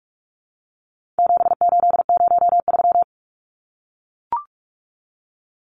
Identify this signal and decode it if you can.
Morse code, heard as a keyed tone, about 44 wpm, 700 Hz. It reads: 7803
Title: Numbers - 4-Digit 45wpm